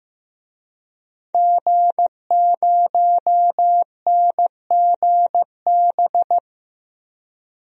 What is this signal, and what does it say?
Morse code, heard as a keyed tone, about 15 wpm, 700 Hz. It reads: G0NGB
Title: DXCC-100 Call Signs - 5 Characters - Repeated 3x - 15wpm